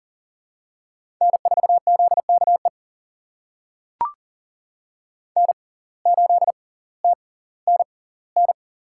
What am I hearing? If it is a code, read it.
Morse code, heard as a keyed tone, about 38 wpm, spaced as for 20 wpm, 700 Hz. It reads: N4ZKE N8TNN
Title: USA Call Signs - 5 Characters - 6x Character Spacing - 40wpm